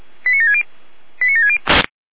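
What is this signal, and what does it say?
Five-tone selective call (ZVEI), heard as quick runs of five tones, tones of about 80 ms. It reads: 89760 89760
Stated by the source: ZVEI five-tone call received by SDR.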